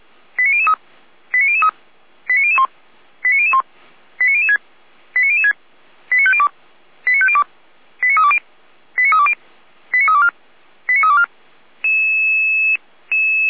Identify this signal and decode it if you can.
Five-tone selective call (ZVEI), heard as quick runs of five tones, tones of about 70 ms. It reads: 89003 89003 89001 89001 89006 89006 89502 89502 89320 89320 89324 89324
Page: /455/SDR/ZVEI/